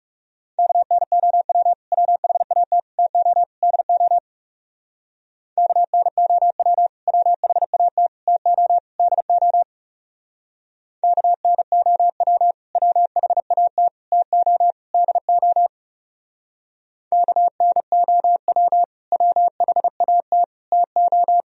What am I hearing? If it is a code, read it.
Morse code, heard as a keyed tone, about 35 wpm, 700 Hz. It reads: KNOW WHAT TO DO KNOW WHAT TO DO KNOW WHAT TO DO KNOW WHAT TO
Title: Speed-Racing - Sets of 4 Words 30wpm